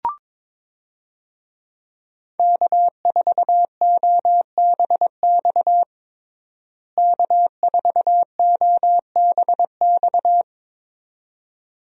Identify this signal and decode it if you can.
Morse code, heard as a keyed tone, about 22 wpm, 700 Hz. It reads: K4OBX K4OBX
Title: USA Call Signs - 5 Characters - Repeated 2x - 22wpm